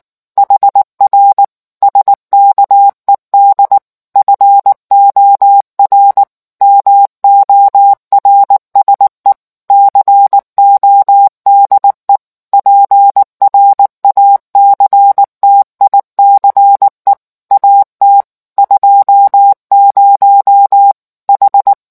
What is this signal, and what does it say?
Morse code, heard as a keyed tone, about 19 wpm, 800 Hz. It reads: HR SKED FOR MORSE CODE PRACTICE AT 20 H